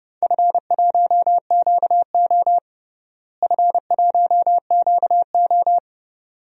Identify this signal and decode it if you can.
Morse code, heard as a keyed tone, about 30 wpm, 700 Hz. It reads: F1QO F1QO
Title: DXCC-100 Call Signs - 4 Characters - Repeated 2x - 30wpm